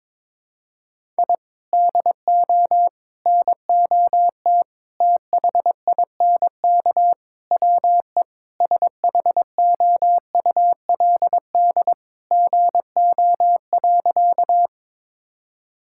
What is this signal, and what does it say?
Morse code, heard as a keyed tone, about 22 wpm, 700 Hz. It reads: I DO NOT THINK WE SHOULD GO.
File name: practice / 22wpm / Sentences with 4-Word Phrases - 22wpm